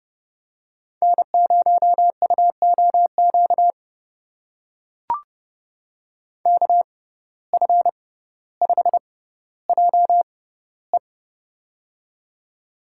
Morse code, heard as a keyed tone, about 30 wpm, spaced as for 15 wpm, 700 Hz. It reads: N0UOQ KF5JE
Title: USA Call Signs - 6x Character Spacing - 30wpm